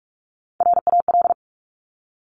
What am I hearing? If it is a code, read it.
Morse code, heard as a keyed tone, about 46 wpm, 700 Hz. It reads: RAL
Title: Top Three Letter Combinations in Words 45wpm